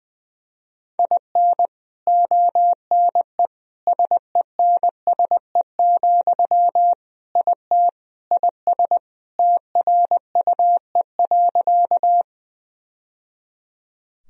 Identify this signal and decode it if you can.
Morse code, heard as a keyed tone, about 20 wpm, 700 Hz. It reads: IN ONE SENSE, IT IS TRUE.
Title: Sentences from Top 1000 Words - Review Part 4 - 20wpm